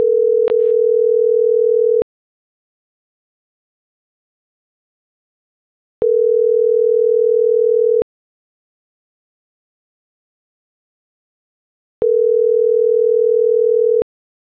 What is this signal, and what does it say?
Ringback tone (North America), 440 + 480 Hz.
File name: CENTRAL SAGOT / out isai.hernandez - .77902